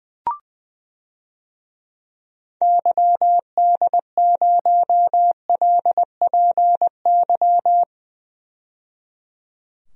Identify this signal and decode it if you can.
Morse code, heard as a keyed tone, about 20 wpm, 700 Hz. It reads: YD0LPY
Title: DXCC-100 Call Signs - 6 Characters - 20wpm